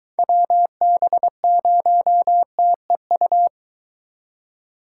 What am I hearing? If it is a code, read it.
Morse code, heard as a keyed tone, about 23 wpm, 700 Hz. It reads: WB0TEU